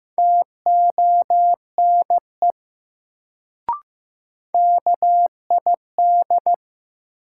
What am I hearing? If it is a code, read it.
Morse code, heard as a keyed tone, about 15 wpm, 700 Hz. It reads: TONE KID
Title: Sentences from Top 1600 Words - New Word Drill Alt - 15wpm